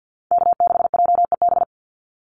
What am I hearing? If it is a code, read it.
Morse code, heard as a keyed tone, about 48 wpm, 700 Hz. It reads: K6JEB
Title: Call Signs - 50wpm